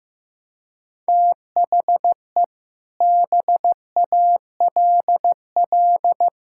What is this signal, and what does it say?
Morse code, heard as a keyed tone, about 15 wpm, 700 Hz. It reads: THE BALL
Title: Sentences from Top 1600 Words 15wpm